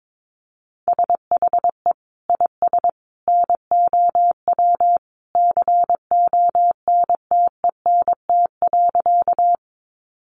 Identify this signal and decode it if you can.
Morse code, heard as a keyed tone, about 22 wpm, 700 Hz. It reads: SHE IS NOW CONTENT.
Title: Sentences from Top 900 Words 22wpm